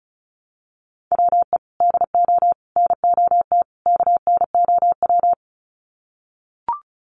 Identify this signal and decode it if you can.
Morse code, heard as a keyed tone, about 34 wpm, 700 Hz. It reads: WE DO NOT KNOW
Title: Sets of 4 Words 35wpm